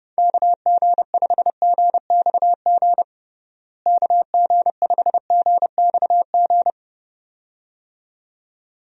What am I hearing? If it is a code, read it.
Morse code, heard as a keyed tone, about 32 wpm, 700 Hz. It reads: KG5GXG KG5GXG
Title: USA Call Signs - 6 Characters - Repeated 2x - 30wpm